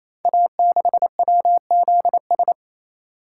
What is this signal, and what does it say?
Morse code, heard as a keyed tone, about 29 wpm, 700 Hz. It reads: A6WZS
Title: Code Groups - 5 Letter:Numbers - 28wpm